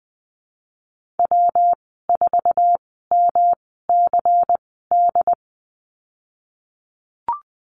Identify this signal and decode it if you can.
Morse code, heard as a keyed tone, about 20 wpm, spaced as for 15 wpm, 700 Hz. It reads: W4MCD
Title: 238 Lesson 38g - Letter q - Call Signs 20wpm